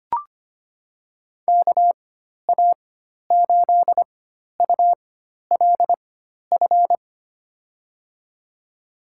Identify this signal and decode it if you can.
Morse code, heard as a keyed tone, about 25 wpm, spaced as for 12 wpm, 700 Hz. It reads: KA8ULF
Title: USA Call Signs - 4x Character Spacing - 25wpm